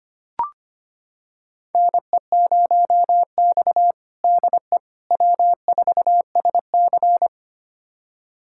Morse code, heard as a keyed tone, about 25 wpm, 700 Hz. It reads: NE0X DE W4SC
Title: USA Callsign DE Callsign - 4 Characters - 25wpm